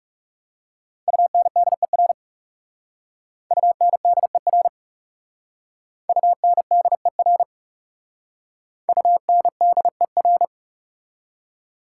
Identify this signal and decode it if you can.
Morse code, heard as a keyed tone, about 40 wpm, 700 Hz. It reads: UNDER UNDER UNDER UNDER